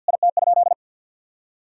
Morse code, heard as a keyed tone, about 51 wpm, 700 Hz. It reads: IT?
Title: Sentences from Top 1000 Words - Review Part 5 - 50wpm